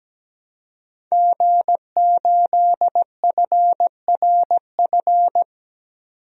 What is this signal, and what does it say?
Morse code, heard as a keyed tone, about 17 wpm, 700 Hz. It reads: G8FRF
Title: DXCC-100 Call Signs - 5 Characters - 17wpm